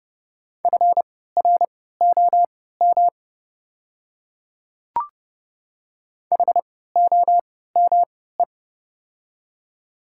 Morse code, heard as a keyed tone, about 30 wpm, spaced as for 17 wpm, 700 Hz. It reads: FROM HOME
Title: 3x Character Spacing - Top 300 Words - 30wpm